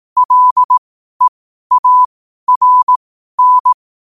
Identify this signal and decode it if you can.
Morse code, heard as a keyed tone, about 18 wpm, spaced as for 12 wpm, 1000 Hz. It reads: LEARN
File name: qotdfiles / 12wpm